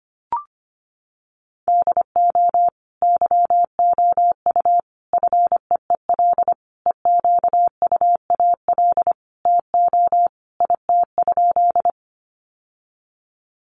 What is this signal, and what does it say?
Morse code, heard as a keyed tone, about 25 wpm, 700 Hz. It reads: DO YOU FEEL EQUAL TO IT?
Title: Sentences from Top 1400 Words 25wpm